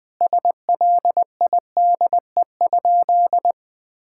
Morse code, heard as a keyed tone, about 20 wpm, 700 Hz. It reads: SLIDE?
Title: Sentences from Top 1600 Words 20wpm